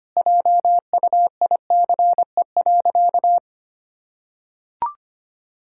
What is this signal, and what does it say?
Morse code, heard as a keyed tone, about 25 wpm, 700 Hz. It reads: JUICE.
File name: Easy Sentences with Hints - 25wpm